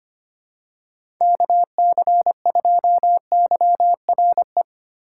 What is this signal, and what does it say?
Morse code, heard as a keyed tone, about 25 wpm, 700 Hz. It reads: KC2YRE
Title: USA Call Signs - 6 Characters - 2x Character Spacing - 25wpm